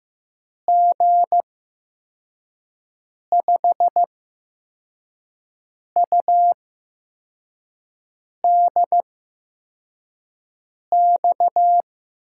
Morse code, heard as a keyed tone, about 15 wpm, spaced as for 4 wpm, 700 Hz. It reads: G5UDX
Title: USA Call Signs - 6 Characters - 8x Character Spacing - 15wpm